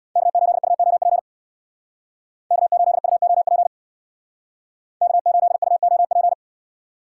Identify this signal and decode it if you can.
Morse code, heard as a keyed tone, about 66 wpm, 700 Hz. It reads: K8UOP K8UOP K8UOP